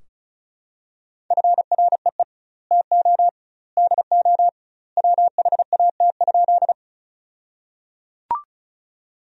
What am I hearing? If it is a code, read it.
Morse code, heard as a keyed tone, about 34 wpm, 700 Hz. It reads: FREE TO DO WHAT?
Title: 2x Word-Spacing - Sentences from Top 500 Words - 35wpm